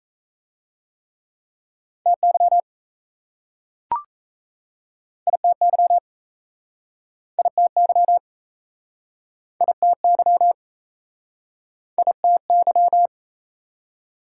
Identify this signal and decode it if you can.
Morse code, heard as a keyed tone, about 39 wpm, 690 Hz. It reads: TY ITY ITY ITY ITY